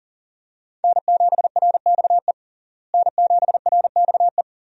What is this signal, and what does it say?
Morse code, heard as a keyed tone, about 40 wpm, 700 Hz. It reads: N7RXE N7RXE
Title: USA Call Signs - 5 Characters - Repeated 2x - 40wpm